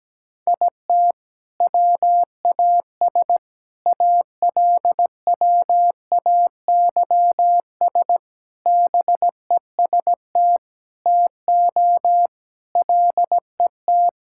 Morse code, heard as a keyed tone, about 17 wpm, 700 Hz. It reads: IT WAS ALWAYS BEST TO LET
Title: Sentences from Top 1000 Words - Review Part 6 - 17wpm